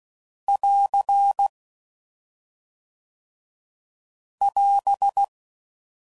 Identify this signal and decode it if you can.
Morse code, heard as a keyed tone, about 16 wpm, 800 Hz. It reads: <AR> <AS>
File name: Sayings / Quotes / HoofInMouth